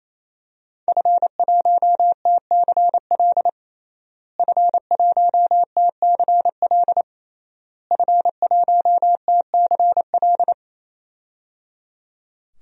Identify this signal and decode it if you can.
Morse code, heard as a keyed tone, about 28 wpm, 700 Hz. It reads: F1TCL F1TCL F1TCL
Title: DXCC-100 Call Signs - 5 Characters - Repeated 3x - 28wpm